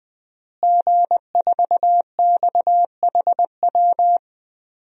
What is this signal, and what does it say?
Morse code, heard as a keyed tone, about 21 wpm, 700 Hz. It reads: G4XHW
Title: DXCC-100 Call Signs - 5 Characters - Repeated 2x - 20wpm